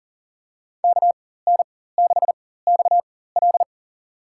Morse code, heard as a keyed tone, about 40 wpm, spaced as for 19 wpm, 700 Hz. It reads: KN6XL
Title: USA Call Signs - 4x Character Spacing - 40wpm